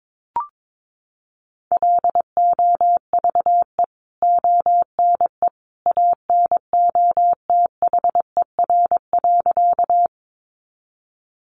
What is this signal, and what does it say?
Morse code, heard as a keyed tone, about 22 wpm, 700 Hz. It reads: LOVE ONE ANOTHER.
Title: Sentences from Top 600 Words 22wpm